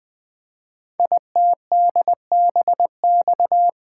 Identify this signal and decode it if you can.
Morse code, heard as a keyed tone, about 20 wpm, 700 Hz. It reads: ITDBX
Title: Code Groups - 5 Letters - 20wpm